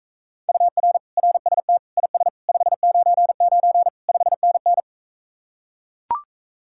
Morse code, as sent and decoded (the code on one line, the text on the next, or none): ..- .-. / .-. ... - / .. ... / ..... ----. ----. / ..... -. -.
UR RST IS 599 5NN